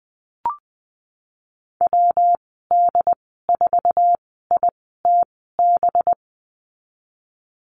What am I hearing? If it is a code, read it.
Morse code, heard as a keyed tone, about 20 wpm, spaced as for 14 wpm, 700 Hz. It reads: WD4ITB
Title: 207 Lesson 33g - Letter j - Call Signs 20wpm